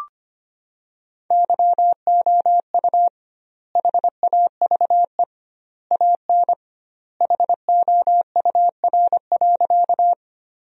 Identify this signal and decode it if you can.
Morse code, heard as a keyed tone, about 25 wpm, 700 Hz. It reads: YOU HAVE AN HOUR.
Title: 2x Word-Spacing - Sentences from Top 400 Words - 25wpm